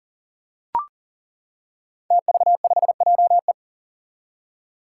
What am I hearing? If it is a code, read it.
Morse code, heard as a keyed tone, about 40 wpm, 700 Hz. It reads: TV5JE
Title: Five Challenging Letter-Number Combinations 40wpm